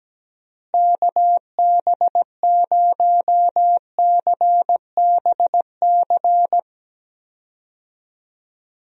Morse code, heard as a keyed tone, about 17 wpm, 700 Hz. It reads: KB0CBC
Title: USA Call Signs - 6 Characters - Repeated 2x - 17wpm